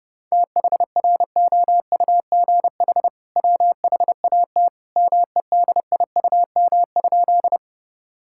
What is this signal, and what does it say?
Morse code, heard as a keyed tone, about 30 wpm, 700 Hz. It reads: THROUGH WHAT MEDIUM?
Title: Sentences from Top 1000 Words - Review Part 6 - 30wpm